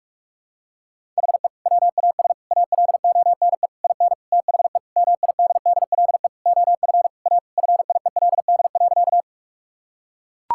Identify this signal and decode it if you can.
Morse code, heard as a keyed tone, about 41 wpm, 700 Hz. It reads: HE WAS ALONE IN THE MIDDLE OF A FIELD.
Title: Sentences from Top 1000 Words - Review Part 6 - 45wpm